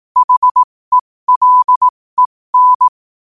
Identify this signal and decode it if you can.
Morse code, heard as a keyed tone, about 18 wpm, spaced as for 15 wpm, 1000 Hz. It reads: HELEN
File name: qotdfiles / 15wpm